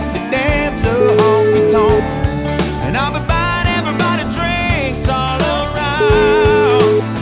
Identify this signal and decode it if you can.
Ringback tone (Europe, 425 Hz), with noise.